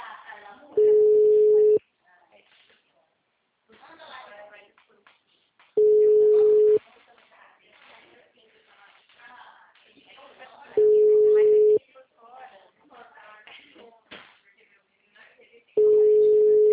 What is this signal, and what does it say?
Ringback tone (Europe, 425 Hz), with speech.